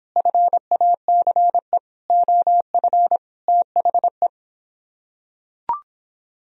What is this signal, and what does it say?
Morse code, heard as a keyed tone, about 26 wpm, 700 Hz. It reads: FACE OF THE